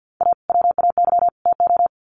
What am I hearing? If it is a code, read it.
Morse code, heard as a keyed tone, about 50 wpm, 700 Hz. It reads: A WAY TO
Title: Sets of 3 Words 50wpm